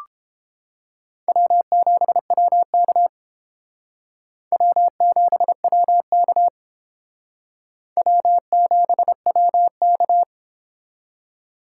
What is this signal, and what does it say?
Morse code, heard as a keyed tone, about 30 wpm, 700 Hz. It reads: W7WK W7WK W7WK